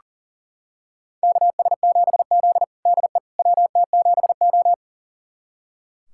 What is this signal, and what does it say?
Morse code, heard as a keyed tone, about 40 wpm, 700 Hz. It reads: KS7Z DE WT7O
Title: USA Callsign DE Callsign - 4 Characters - 40wpm